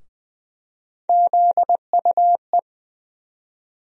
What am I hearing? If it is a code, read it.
Morse code, heard as a keyed tone, about 20 wpm, 700 Hz. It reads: ZUE
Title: USA Call Sign Postfixes - 3 Letters 20wpm